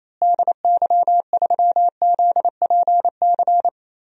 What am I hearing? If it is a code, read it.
Morse code, heard as a keyed tone, about 28 wpm, 700 Hz. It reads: DY3ZPC
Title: DXCC-100 Call Signs - 6 Characters - Repeated 3x - 28wpm